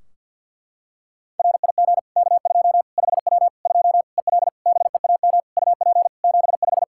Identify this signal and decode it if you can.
Morse code, heard as a keyed tone, about 50 wpm, 700 Hz. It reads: RIG K2 5W 2 EL BEAM UP 75
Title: Practice QSO - Exchange 2 - Info - 50wpm